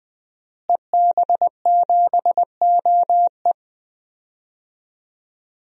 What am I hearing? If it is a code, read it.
Morse code, heard as a keyed tone, about 20 wpm, 700 Hz. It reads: EB7OE
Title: DXCC-100 Call Signs - 5 Characters - Repeated 2x - 20wpm